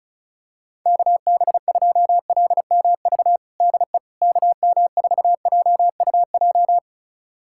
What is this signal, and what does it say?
Morse code, heard as a keyed tone, about 34 wpm, 700 Hz. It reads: KB2LMV DE KM4JUJ
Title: USA Callsign DE Callsign - 6 Characters - 35wpm